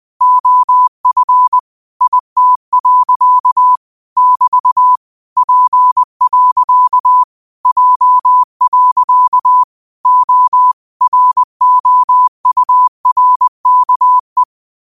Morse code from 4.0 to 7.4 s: <BT> P.

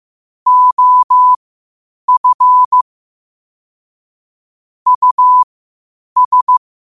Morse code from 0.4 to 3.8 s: OF

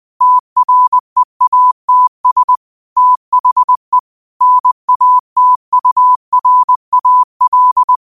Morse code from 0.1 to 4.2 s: TREATS THE

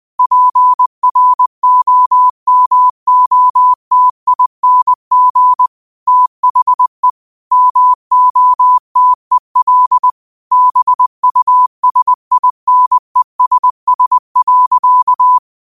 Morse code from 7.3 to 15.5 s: MOTEL BUSINESS.

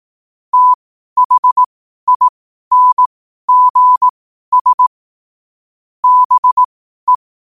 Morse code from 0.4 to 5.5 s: THINGS

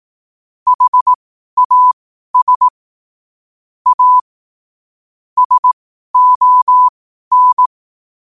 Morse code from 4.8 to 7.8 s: SON

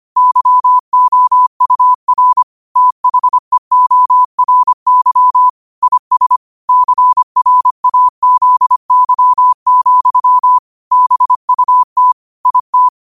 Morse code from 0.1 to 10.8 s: YOUR THEORY IS CRAZY,

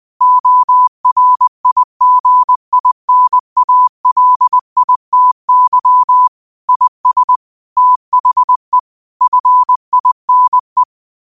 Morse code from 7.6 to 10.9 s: THE FINE

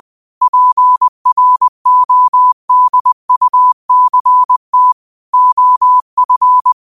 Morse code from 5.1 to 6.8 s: OF

